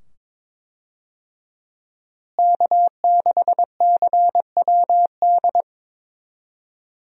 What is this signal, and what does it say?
Morse code, heard as a keyed tone, about 22 wpm, 700 Hz. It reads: K6CWD
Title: USA Call Signs - 5 Characters - Repeated 2x - 22wpm